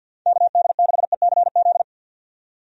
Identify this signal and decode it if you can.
Morse code, heard as a keyed tone, about 53 wpm, 700 Hz. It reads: KD6ECZ
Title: USA Call Signs - 6 Characters - 50wpm